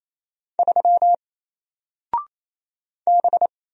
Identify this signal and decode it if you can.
Morse code, heard as a keyed tone, about 28 wpm, 700 Hz. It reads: 3 B
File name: Single-Letter-Number-28wpm